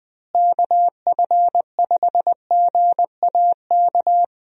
Morse code, heard as a keyed tone, about 21 wpm, 700 Hz. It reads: KF5GAK
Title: USA Call Signs - 6 Characters - Repeated 3x - 20wpm